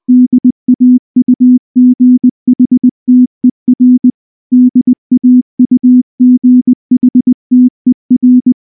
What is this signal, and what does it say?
Morse code, heard as a keyed tone, about 19 wpm, 250 Hz. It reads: DAUGHTER DAUGHTER
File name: morse / morse-training-20WPM-5